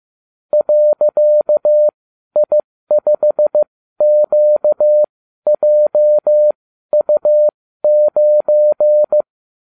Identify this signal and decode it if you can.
Morse code, heard as a keyed tone, about 15 wpm, spaced as for 12 wpm, 600 Hz. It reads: .I5QJU9